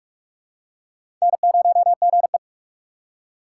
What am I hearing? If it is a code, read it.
Morse code, heard as a keyed tone, about 46 wpm, 700 Hz. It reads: N0GE
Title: USA Call Signs - 4 Characters - 6x Character Spacing - 45wpm